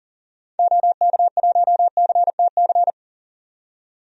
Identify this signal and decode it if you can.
Morse code, heard as a keyed tone, about 40 wpm, 700 Hz. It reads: OK1CTC
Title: DXCC-100 Call Signs - 6 Characters - 40wpm